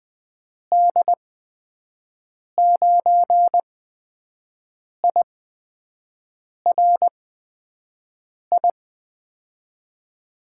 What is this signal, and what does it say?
Morse code, heard as a keyed tone, about 21 wpm, spaced as for 6 wpm, 700 Hz. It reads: D9IRI
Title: USA Call Signs - 6 Characters - 8x Character Spacing - 20wpm